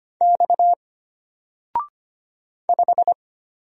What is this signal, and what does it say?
Morse code, heard as a keyed tone, about 25 wpm, 700 Hz. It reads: X 5
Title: Single Letter-Number Alt 25wpm